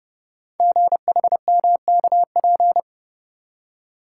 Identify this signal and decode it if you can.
Morse code, heard as a keyed tone, about 28 wpm, 700 Hz. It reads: GHMKP
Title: Code Groups - 5 Letters - 30wpm